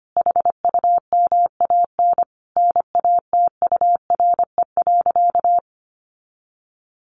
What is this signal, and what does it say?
Morse code, heard as a keyed tone, about 25 wpm, 700 Hz. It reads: HUMAN NATURE.
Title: Sentences from Top 700 Words 25wpm